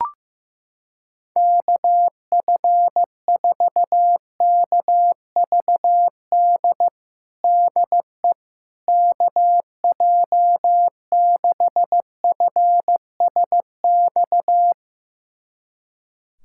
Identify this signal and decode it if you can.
Morse code, heard as a keyed tone, about 15 wpm, 700 Hz. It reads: KF4KVD DE KJ6FSX
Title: USA Callsign DE Callsign - 6 Characters - 15wpm